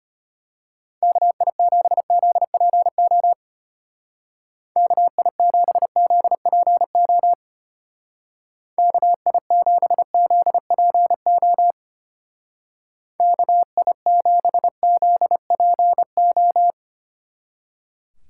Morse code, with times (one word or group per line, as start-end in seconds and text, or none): 1.0-3.3 KI7ZPO
4.8-7.3 KI7ZPO
8.8-11.7 KI7ZPO
13.2-16.7 KI7ZPO